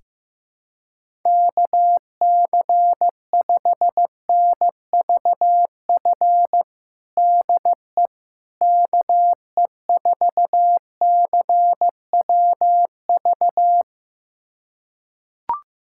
Morse code, heard as a keyed tone, about 15 wpm, 700 Hz. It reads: KC5NVF DE KE4CWV
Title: USA Callsign DE Callsign 15wpm